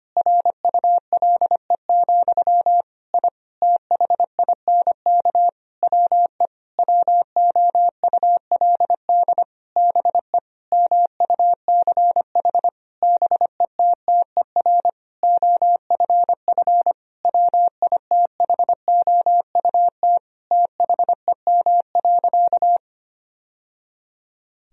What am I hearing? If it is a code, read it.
Morse code, heard as a keyed tone, about 25 wpm, 700 Hz. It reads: RULE, I THINK WE WOULD BE MUCH BETTER OFF WITHOUT THEM.
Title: Sentences from Top 1000 Words - Review Part 3 - 25wpm